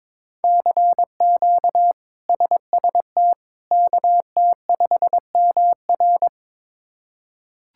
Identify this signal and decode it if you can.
Morse code, heard as a keyed tone, about 22 wpm, 700 Hz. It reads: CQ SST KT5MR
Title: K1USN SST QSO Practice - 22wpm